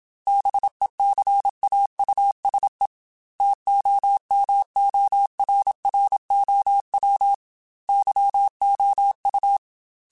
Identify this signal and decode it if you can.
Morse code, heard as a keyed tone, about 27 wpm, 800 Hz. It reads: BECAUSE TOMORROW YOU
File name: Chuckles / Chuckles-mp3s / SoftWords @25